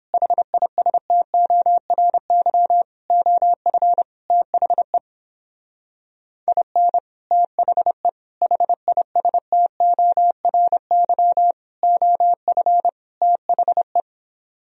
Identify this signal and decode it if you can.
Morse code, heard as a keyed tone, about 29 wpm, 700 Hz. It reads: HISTORY OF THE IN THE HISTORY OF THE